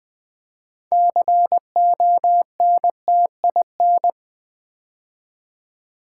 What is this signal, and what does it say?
Morse code, heard as a keyed tone, about 20 wpm, 700 Hz. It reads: CONTIN
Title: Build A Word - 2 Letters At A Time - 8-Letter Words - 20wpm